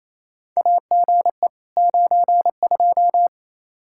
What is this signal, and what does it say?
Morse code, heard as a keyed tone, about 28 wpm, 700 Hz. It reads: AGE 92
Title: Practice QSO - Exchange 3 - More Info - 28wpm